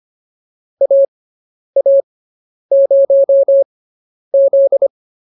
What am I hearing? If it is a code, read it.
Morse code, heard as a keyed tone, about 25 wpm, spaced as for 10 wpm, 550 Hz. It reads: AA0Z
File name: POTA201 10